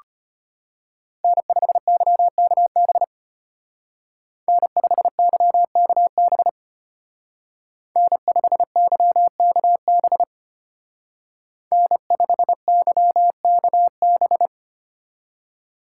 Morse code, heard as a keyed tone, about 32 wpm, 700 Hz. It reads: N5YKB N5YKB N5YKB N5YKB